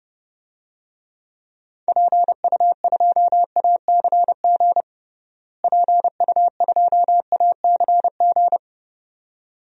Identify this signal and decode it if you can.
Morse code, heard as a keyed tone, about 30 wpm, 700 Hz. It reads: PU2ACG PU2ACG
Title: DXCC-100 Call Signs - 6 Characters - Repeated 2x - 30wpm